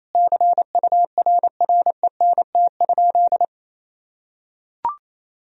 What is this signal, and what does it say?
Morse code, heard as a keyed tone, about 27 wpm, 700 Hz. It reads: CURRENT?
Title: Sentences from Top 1000 Words - Review Part 4 - 28wpm